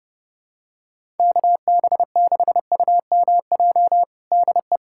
Morse code, heard as a keyed tone, about 30 wpm, 700 Hz. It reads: KB6UMJ DE
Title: USA Callsign DE Callsign - 6 Characters - 30wpm